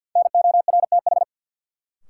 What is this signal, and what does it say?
Morse code, heard as a keyed tone, about 53 wpm, 700 Hz. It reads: NORTH
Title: Sentences from Top 400 Words - New Word Drill - 50wpm